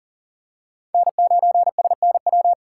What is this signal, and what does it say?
Morse code, heard as a keyed tone, about 40 wpm, 700 Hz. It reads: N9SNW
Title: USA Call Signs - 5 Characters - 3x Character Spacing - 40wpm